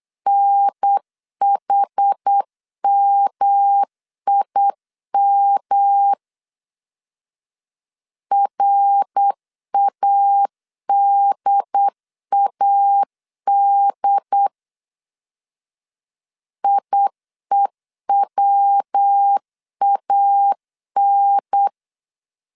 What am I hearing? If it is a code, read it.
Morse code, heard as a keyed tone, about 8 wpm, 800 Hz. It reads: NHMIM RADAD IEWAN
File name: CW / DET / 1 3 1500C S B 8wpm 4141